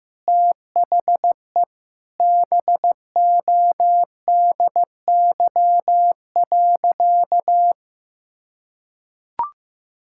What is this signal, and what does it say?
Morse code, heard as a keyed tone, about 15 wpm, 700 Hz. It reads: THE BODY.
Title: Sentences from Top 1000 Words - Review Part 6 - 15wpm